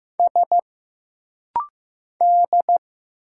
Morse code, heard as a keyed tone, about 15 wpm, 700 Hz. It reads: S D
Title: Single Letter-Number Alt - Rapid-Fire 15wpm